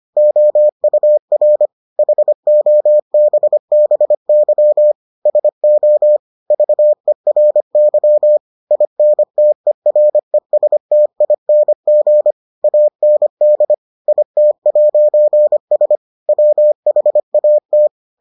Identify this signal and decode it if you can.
Morse code, heard as a keyed tone, about 25 wpm, 600 Hz. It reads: OUR HOBBY SO VERY INTERESTING AND IT'S WHAT